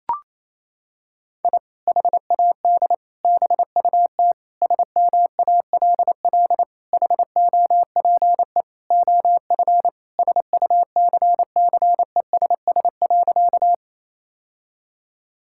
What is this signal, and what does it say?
Morse code, heard as a keyed tone, about 28 wpm, 700 Hz. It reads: I HAD BUT SMALL HOPE OF SUCCESS.
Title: Sentences from Top 1000 Words - Review Part 6 - 28wpm